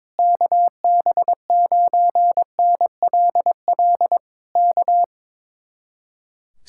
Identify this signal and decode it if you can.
Morse code, heard as a keyed tone, about 22 wpm, 700 Hz. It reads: KB9NLL K
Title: Speed-Racing Common QSO Phrases - 22wpm